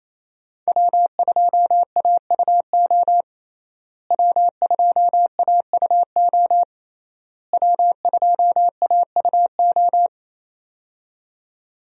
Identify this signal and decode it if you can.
Morse code, heard as a keyed tone, about 27 wpm, 700 Hz. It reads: W2AUO W2AUO W2AUO
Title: USA Call Signs - 5 Characters - Repeated 3x - 28wpm